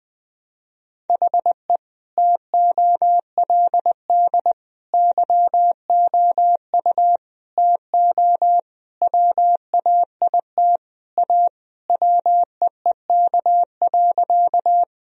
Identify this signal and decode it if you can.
Morse code, heard as a keyed tone, about 21 wpm, 700 Hz. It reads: HE TOLD YOU TO WAIT A WEEK.
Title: Sentences from Top 1000 Words - Review Part 2 - 20wpm